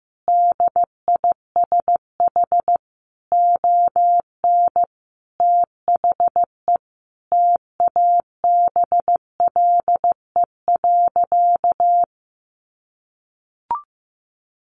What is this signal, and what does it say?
Morse code, heard as a keyed tone, about 15 wpm, 700 Hz. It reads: DISH ON THE TABLE.
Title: Sentences from Top 1300 Words 15wpm